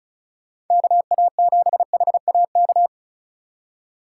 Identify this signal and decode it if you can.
Morse code, heard as a keyed tone, about 34 wpm, 700 Hz. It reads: KA7HAK
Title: USA Call Signs - 6 Characters - 2x Character Spacing - 35wpm